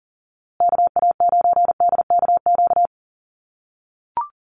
Morse code, heard as a keyed tone, about 40 wpm, 700 Hz. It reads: KA9DKQ
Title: USA Call Signs Repeated 3x 40wpm